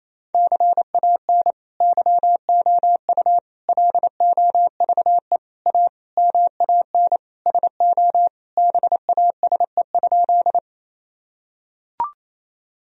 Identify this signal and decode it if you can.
Morse code, heard as a keyed tone, about 28 wpm, 700 Hz. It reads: CAN YOU LOVE A MAN SO BASE?
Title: Sentences from Top 1000 Words - Review Part 1 - 28wpm